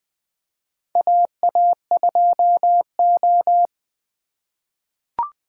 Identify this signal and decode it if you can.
Morse code, heard as a keyed tone, about 20 wpm, 700 Hz. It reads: AA2O
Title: USA Call Signs - 4 Characters - 4x Character Spacing - 20wpm